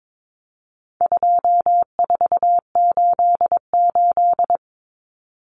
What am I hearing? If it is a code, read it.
Morse code, heard as a keyed tone, about 22 wpm, 700 Hz. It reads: 2488
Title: Numbers - 4-Digit 22wpm